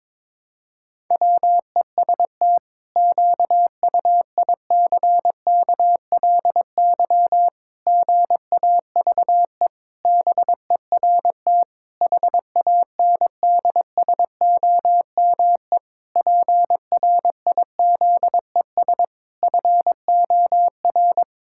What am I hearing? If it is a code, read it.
Morse code, heard as a keyed tone, about 22 wpm, 700 Hz. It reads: WEST QUICKLY GAVE BERT HANDSOME PRIZES FOR
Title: Pangram Challenge v2 - 22wpm